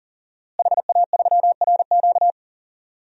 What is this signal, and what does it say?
Morse code, heard as a keyed tone, about 38 wpm, 700 Hz. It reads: HA3RQ
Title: DXCC-100 Call Signs - 5 Characters - Repeated 2x - 40wpm